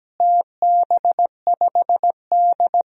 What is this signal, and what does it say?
Morse code, heard as a keyed tone, about 17 wpm, 700 Hz. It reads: TB5D
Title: DXCC-100 Call Signs - 4 Characters - Repeated 3x - 17wpm